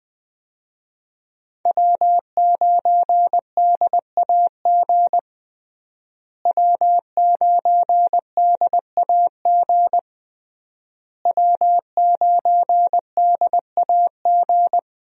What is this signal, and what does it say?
Morse code, heard as a keyed tone, about 20 wpm, 700 Hz. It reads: W9DAG W9DAG W9DAG
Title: USA Call Signs - 5 Characters - Repeated 3x - 20wpm